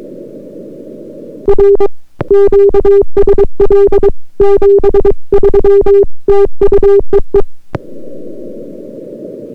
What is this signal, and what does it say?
Morse code, heard as a keyed tone, about 23 wpm, 400 Hz. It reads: R QSL 73 TU EE